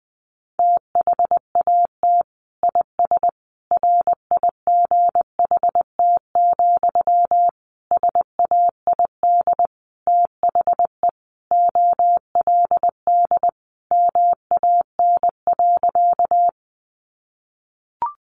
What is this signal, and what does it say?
Morse code, heard as a keyed tone, about 20 wpm, 700 Hz. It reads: THAT IS RIGHT, SAID THE OLD MAN.
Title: Sentences from Top 200 Words 20wpm